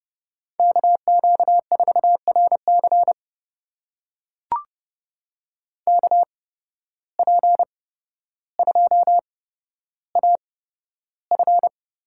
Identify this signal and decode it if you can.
Morse code, heard as a keyed tone, about 30 wpm, spaced as for 13 wpm, 700 Hz. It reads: KQ4RC KP2AF
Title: USA Call Signs - 5 Characters - 8x Character Spacing - 30wpm